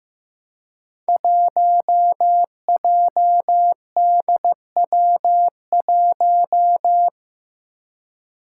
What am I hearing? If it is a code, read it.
Morse code, heard as a keyed tone, about 15 wpm, 700 Hz. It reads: 1JDW1
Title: Five Challenging Letter-Number Combinations 15wpm